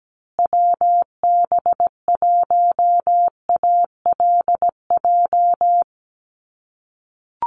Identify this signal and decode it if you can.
Morse code, heard as a keyed tone, about 17 wpm, 700 Hz. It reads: WB1ALJ
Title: USA Call Signs Repeated 2x 17wpm